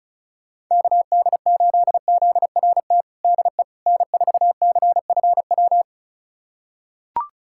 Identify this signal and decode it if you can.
Morse code, heard as a keyed tone, about 36 wpm, 700 Hz. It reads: KD8ZRT DE N4CFW
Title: USA Callsign DE Callsign 35wpm